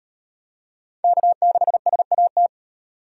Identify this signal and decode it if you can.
Morse code, heard as a keyed tone, about 38 wpm, 700 Hz. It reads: K6SAT